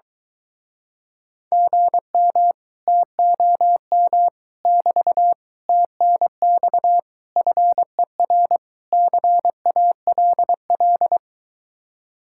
Morse code, heard as a keyed tone, about 23 wpm, 700 Hz. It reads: GM TOM <BT> TNX FER CALL